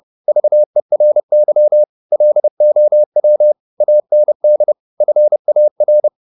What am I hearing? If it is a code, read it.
Morse code, heard as a keyed tone, about 30 wpm, 600 Hz. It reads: VERY LOW AND FAR
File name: the old man and the sea / the old man and the sea 1 30wpm